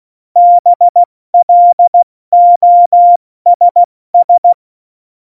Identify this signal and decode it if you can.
Morse code, heard as a keyed tone, about 16 wpm, 700 Hz. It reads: BLOSS
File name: Morsekurs / Witze / Witz 27. November 2018, 70 F80